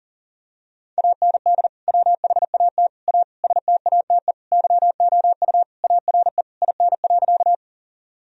Morse code, heard as a keyed tone, about 40 wpm, 700 Hz. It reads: AND WHAT A STATE YOU ARE IN.
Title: Sentences from Top 1000 Words - Review Part 2 - 40wpm